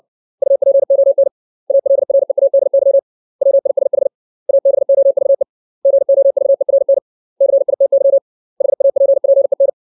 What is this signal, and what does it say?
Morse code, heard as a keyed tone, about 60 wpm, 550 Hz. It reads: UPON ALREADY WISH ABOVE GOVERN CITY HAPPEN
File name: 632 words / 632 Words 60WPM